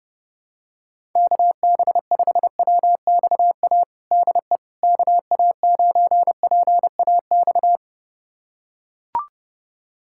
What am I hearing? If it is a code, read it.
Morse code, heard as a keyed tone, about 32 wpm, 700 Hz. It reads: KB5WXA DE KA9PAX
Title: USA Callsign DE Callsign - 6 Characters - 30wpm